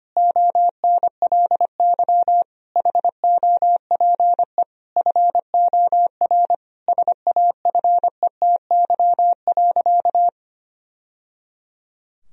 Morse code, heard as a keyed tone, about 25 wpm, 700 Hz. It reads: ONLY HOPE FOR SAFETY.
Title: Sentences from Top 1000 Words - Review Part 4 - 25wpm